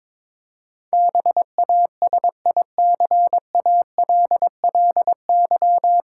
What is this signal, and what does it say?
Morse code, heard as a keyed tone, about 22 wpm, 700 Hz. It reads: BASICALLY
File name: 9 Letter Words - First 3 Letters Given - 22wpm